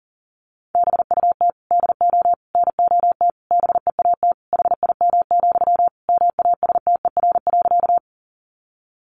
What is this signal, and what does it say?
Morse code, heard as a keyed tone, about 40 wpm, 700 Hz. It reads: BUT DO NOT BEAT HIM, MASTER.
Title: Sentences from Top 1300 Words 40wpm